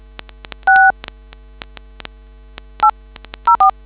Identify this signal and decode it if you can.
DTMF, heard as telephone touch tones, tones of about 110 ms. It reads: 6 804